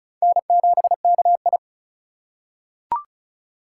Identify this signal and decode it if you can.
Morse code, heard as a keyed tone, about 35 wpm, 700 Hz. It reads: N7KI
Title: USA Call Signs - 4 Characters - 2x Character Spacing - 35wpm